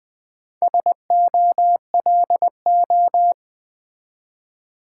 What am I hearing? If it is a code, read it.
Morse code, heard as a keyed tone, about 20 wpm, 700 Hz. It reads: SOLO
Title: Build A Word - 1 Letter At A Time - 4-Letter Words - 20wpm